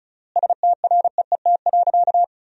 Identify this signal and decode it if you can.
Morse code, heard as a keyed tone, about 36 wpm, 700 Hz. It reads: STREET.
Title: 3x Word-Spacing - Sentences from Top 500 Words - 35wpm